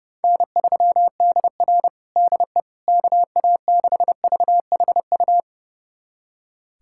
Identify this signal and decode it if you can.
Morse code, heard as a keyed tone, about 30 wpm, 700 Hz. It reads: N3DR DE KA6VHU
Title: USA Callsign DE Callsign 30wpm